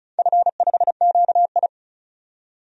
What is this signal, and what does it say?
Morse code, heard as a keyed tone, about 36 wpm, 700 Hz. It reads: F5QI
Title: DXCC-100 Call Signs - 4 Characters - 35wpm